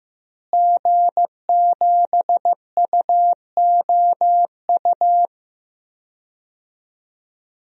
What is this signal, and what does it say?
Morse code, heard as a keyed tone, about 15 wpm, 700 Hz. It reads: G7UOU
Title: DXCC-100 Call Signs - 5 Characters - Repeated 3x - 15wpm